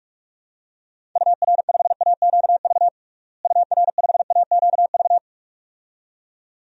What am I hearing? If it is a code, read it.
Morse code, heard as a keyed tone, about 45 wpm, 700 Hz. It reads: UR5AQV UR5AQV
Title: DXCC-100 Call Signs - 6 Characters - Repeated 2x - 45wpm